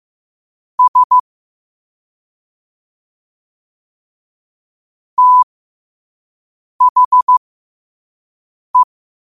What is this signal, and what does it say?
Morse code, heard as a keyed tone, about 15 wpm, spaced as for 5 wpm, 1000 Hz. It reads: S THE
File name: qotdfiles / 5wpm